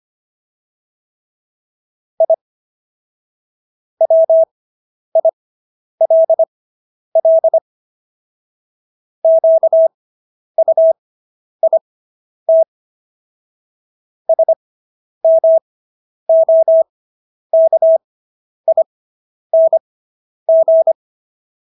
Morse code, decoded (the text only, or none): I WILL QUIT SMOKING